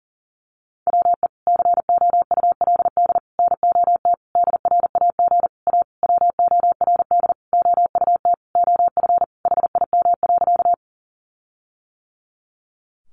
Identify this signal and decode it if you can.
Morse code, heard as a keyed tone, about 40 wpm, 700 Hz. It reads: WE COULD NOT DRAG A WORD OUT OF HIM.
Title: Sentences from Top 1500 Words 40wpm